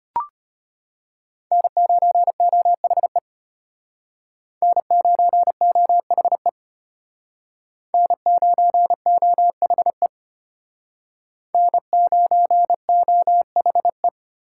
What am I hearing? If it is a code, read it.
Morse code, heard as a keyed tone, about 32 wpm, 700 Hz. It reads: N9OHE N9OHE N9OHE N9OHE